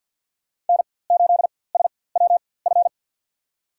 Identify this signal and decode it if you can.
Morse code, heard as a keyed tone, about 47 wpm, spaced as for 23 wpm, 700 Hz. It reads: N8SWF
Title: USA Call Signs - 4x Character Spacing - 50wpm